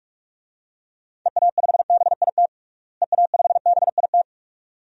Morse code, heard as a keyed tone, about 46 wpm, 700 Hz. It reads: EA5BIT EA5BIT
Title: DXCC-100 Call Signs - 6 Characters - Repeated 2x - 45wpm